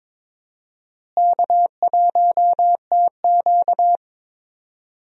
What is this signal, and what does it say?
Morse code, heard as a keyed tone, about 21 wpm, 700 Hz. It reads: K1TQ
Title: USA Call Signs - 4 Characters - Repeated 2x - 22wpm